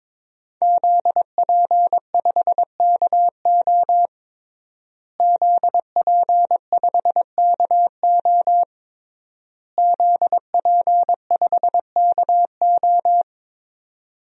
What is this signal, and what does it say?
Morse code, heard as a keyed tone, about 22 wpm, 700 Hz. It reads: ZP5KO ZP5KO ZP5KO
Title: DXCC-100 Call Signs - 5 Characters - Repeated 3x - 22wpm